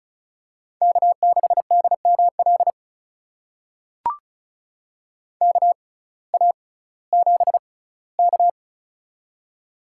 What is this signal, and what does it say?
Morse code, heard as a keyed tone, about 36 wpm, spaced as for 19 wpm, 700 Hz. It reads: K6DML KA7K
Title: USA Call Signs - 6x Character Spacing - 35wpm